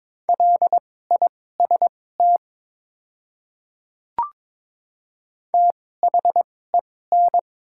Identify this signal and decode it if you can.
Morse code, heard as a keyed tone, about 22 wpm, spaced as for 16 wpm, 700 Hz. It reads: LIST THEN
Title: 2x Character Spacing - Top 500 Words - 22wpm